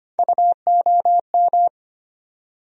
Morse code, heard as a keyed tone, about 25 wpm, 700 Hz. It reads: UOM
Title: USA Call Sign Postfixes - 3 Letters 25wpm